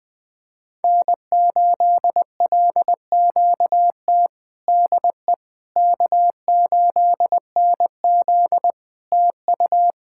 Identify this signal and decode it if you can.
Morse code, heard as a keyed tone, about 20 wpm, 700 Hz. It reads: N8LQT DE K8NZ TU
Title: Practice QSO - Exchange 4 - Ending - 20wpm